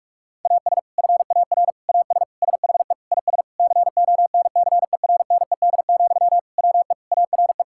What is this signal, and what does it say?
Morse code, heard as a keyed tone, about 41 wpm, 700 Hz. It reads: AS FAR AS SHE IS CONCERNED, WE ARE
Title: Sentences from Top 1000 Words - Review Part 6 - 45wpm